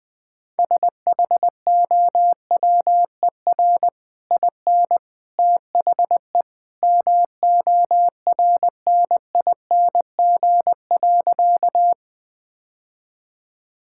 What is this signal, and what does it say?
Morse code, heard as a keyed tone, about 20 wpm, 700 Hz. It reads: SHOWER IN THE MORNING.
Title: Easy Sentences with Hints - 20wpm